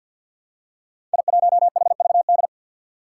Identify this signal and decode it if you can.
Morse code, heard as a keyed tone, about 53 wpm, 700 Hz. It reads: I1HVD
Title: Five Challenging Letter-Number Combinations 50wpm